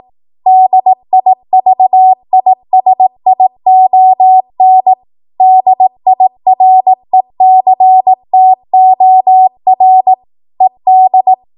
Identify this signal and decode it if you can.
Morse code, heard as a keyed tone, about 18 wpm, 750 Hz. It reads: DIVISION DIRECTOR ED